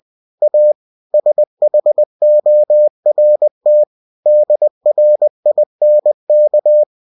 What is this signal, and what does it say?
Morse code, heard as a keyed tone, about 20 wpm, 600 Hz. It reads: A SHORT DRINK
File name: cw-books / winnie-the-pooh / winnie-20wpm-page-0100